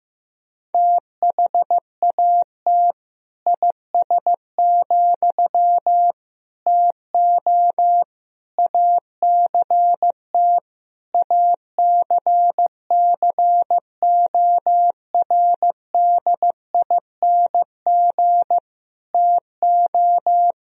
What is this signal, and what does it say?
Morse code, heard as a keyed tone, about 15 wpm, 700 Hz. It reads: THAT IS, TO ACT ACCORDING TO
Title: Sentences from Top 1000 Words - Review Part 4 - 15wpm